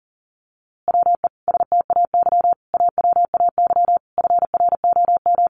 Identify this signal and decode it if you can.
Morse code, heard as a keyed tone, about 40 wpm, 700 Hz. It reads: WE STAY AWAY FROM
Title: A Sound Of Thunder 40wpm